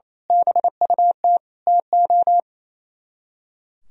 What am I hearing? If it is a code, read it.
Morse code, heard as a keyed tone, about 29 wpm, 700 Hz. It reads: BUT TO
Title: Speed-Racing - Sets of 5 Words 28wpm